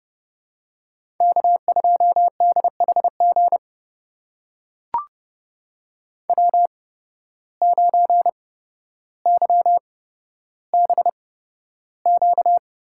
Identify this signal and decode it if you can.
Morse code, heard as a keyed tone, about 30 wpm, spaced as for 13 wpm, 700 Hz. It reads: K2DHG W9YBQ
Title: USA Call Signs - 5 Characters - 8x Character Spacing - 30wpm